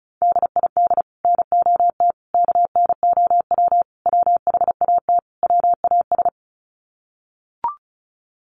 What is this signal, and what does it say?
Morse code, heard as a keyed tone, about 34 wpm, 700 Hz. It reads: DID NOT KNOW WHAT WAS
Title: Sets of 5 Words Encore2 35wpm